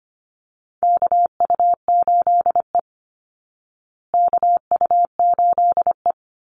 Morse code, heard as a keyed tone, about 25 wpm, 700 Hz. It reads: KU8E KU8E
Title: Call Signs Repeated 3x - 25wpm